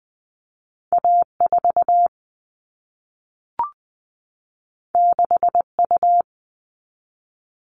Morse code, heard as a keyed tone, about 20 wpm, 700 Hz. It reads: A4 6U
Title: All Cut Numbers - 2-Digit 20wpm